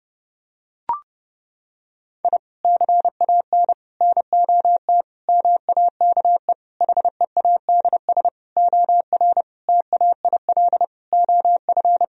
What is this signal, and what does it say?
Morse code, heard as a keyed tone, about 32 wpm, 700 Hz. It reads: I CAN NOT MAKE HEADS OR TAIL OF
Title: Sentences from Top 1000 Words - Review Part 3 - 30wpm